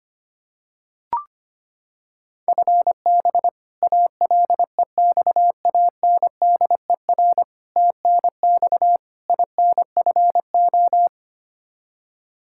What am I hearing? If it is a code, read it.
Morse code, heard as a keyed tone, about 25 wpm, 700 Hz. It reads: FB ALEXANDER TNX INFO
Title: K1USN SST QSO Practice 25wpm